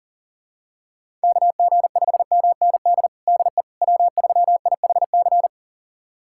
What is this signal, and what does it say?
Morse code, heard as a keyed tone, about 40 wpm, 700 Hz. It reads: KG5MND DE W3IHC
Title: USA Callsign DE Callsign 40wpm